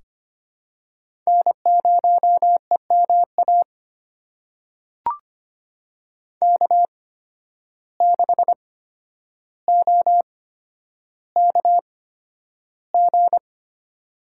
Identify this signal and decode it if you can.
Morse code, heard as a keyed tone, about 25 wpm, spaced as for 11 wpm, 700 Hz. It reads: N0EMA K6OKG
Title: USA Call Signs - 5 Characters - 8x Character Spacing - 25wpm